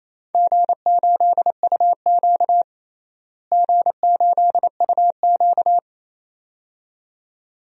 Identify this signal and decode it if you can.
Morse code, heard as a keyed tone, about 28 wpm, 700 Hz. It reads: G8UQ G8UQ
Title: DXCC-100 Call Signs - 4 Characters - Repeated 2x - 28wpm